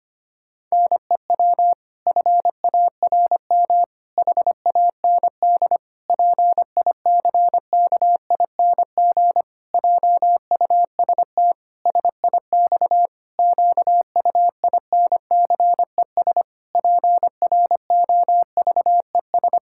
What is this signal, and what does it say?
Morse code, heard as a keyed tone, about 25 wpm, 700 Hz. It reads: NEW FARM HAND PICKING JUST SIX QUINCES PROVES
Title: Pangram Challenge v2 - 25wpm